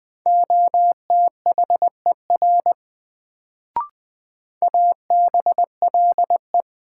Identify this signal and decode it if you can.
Morse code, heard as a keyed tone, about 20 wpm, 700 Hz. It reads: OTHER ABLE
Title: Sets of 2 Words - New Word Drill - 20wpm